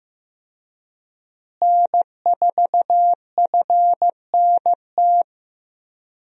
Morse code, heard as a keyed tone, about 15 wpm, 700 Hz. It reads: N4FNT
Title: USA Call Signs - 5 Characters - Repeated 2x - 15wpm